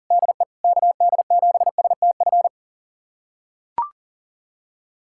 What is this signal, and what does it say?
Morse code, heard as a keyed tone, about 40 wpm, 700 Hz. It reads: DE KD7STF
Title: USA Callsign DE Callsign 40wpm